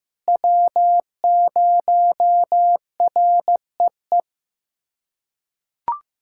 Morse code, heard as a keyed tone, about 15 wpm, 700 Hz. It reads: W0REE
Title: USA Callsign DE Callsign - 5 Characters - 15wpm